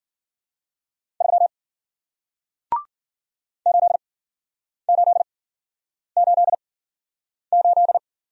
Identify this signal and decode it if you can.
Morse code, heard as a keyed tone, about 53 wpm, spaced as for 9 wpm, 700 Hz. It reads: 3 8888